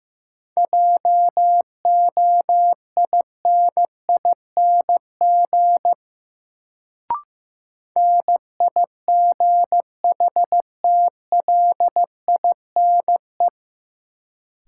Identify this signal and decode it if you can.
Morse code, heard as a keyed tone, about 15 wpm, 700 Hz. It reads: JOINING NIGHTLINE
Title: Sets of 5 Words - New Word Drill - 15wpm